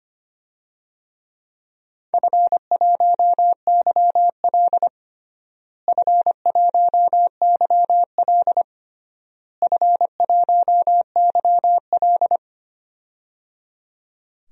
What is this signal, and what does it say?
Morse code, heard as a keyed tone, about 25 wpm, 700 Hz. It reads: F1YL F1YL F1YL
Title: DXCC-100 Call Signs - 4 Characters - Repeated 3x - 25wpm